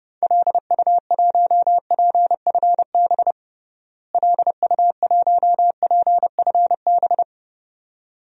Morse code, heard as a keyed tone, about 32 wpm, 700 Hz. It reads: LU1PFB LU1PFB
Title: DXCC-100 Call Signs - 6 Characters - Repeated 2x - 30wpm